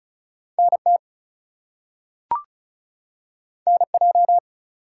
Morse code, heard as a keyed tone, about 36 wpm, 700 Hz. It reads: NT NJ
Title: US States and Canadian Provinces 35wpm